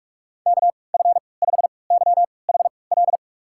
Morse code, heard as a keyed tone, about 46 wpm, spaced as for 25 wpm, 700 Hz. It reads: KF5YHL
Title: USA Call Signs - 6 Characters - 3x Character Spacing - 45wpm